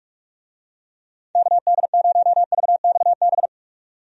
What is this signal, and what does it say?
Morse code, heard as a keyed tone, about 46 wpm, 700 Hz. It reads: KD0VXB
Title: USA Call Signs - 6 Characters - 45wpm